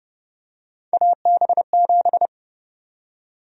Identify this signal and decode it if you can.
Morse code, heard as a keyed tone, about 30 wpm, 700 Hz. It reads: AB7
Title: USA Call Sign Prefixes - Repeated 2x - 30wpm